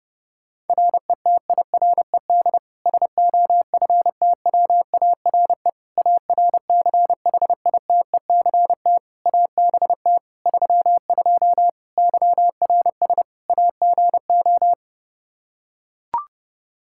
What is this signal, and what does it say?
Morse code, heard as a keyed tone, about 30 wpm, 700 Hz. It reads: RETIRED SOFTWARE ARCHITECT ABT 32 YRS AGO
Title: Practice QSO - Exchange 3 - More Info - 30wpm